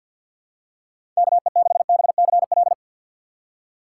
Morse code, heard as a keyed tone, about 46 wpm, 700 Hz. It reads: KE6BCL
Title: USA Call Signs - 3x Character Spacing - 50wpm